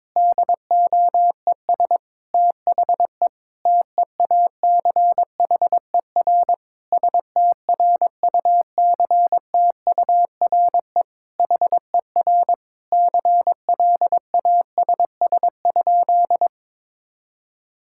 Morse code, heard as a keyed tone, about 22 wpm, 700 Hz. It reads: DOES THE TEACHER STRUCTURE HER CLASS?
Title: Sentences from Top 1000 Words - Review Part 6 - 22wpm